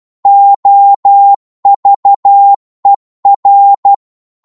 Morse code, heard as a keyed tone, about 12 wpm, 800 Hz. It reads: OVER
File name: Morse / psg-12wpm